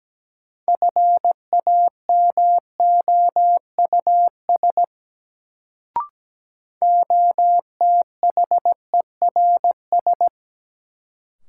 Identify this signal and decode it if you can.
Morse code, heard as a keyed tone, about 17 wpm, 700 Hz. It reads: FAMOUS OTHERS
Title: Sentences from Top 1500 Words - New Word Drill Alt - 17wpm